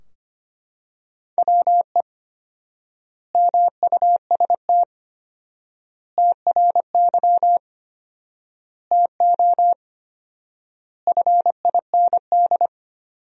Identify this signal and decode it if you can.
Morse code, heard as a keyed tone, about 25 wpm, 700 Hz. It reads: WE MUST TRY TO FIND
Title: 4x Word-Spacing - Sentences from Top 400 Words - 25wpm